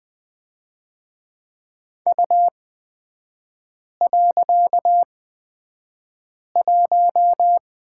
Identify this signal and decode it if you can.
Morse code, heard as a keyed tone, about 20 wpm, spaced as for 5 wpm, 700 Hz. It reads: U.1
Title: 255 Lesson 41e - Period - Three Characters 20wpm